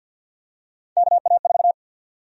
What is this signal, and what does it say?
Morse code, heard as a keyed tone, about 48 wpm, 700 Hz. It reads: KA4
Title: USA Call Sign Prefixes - 50wpm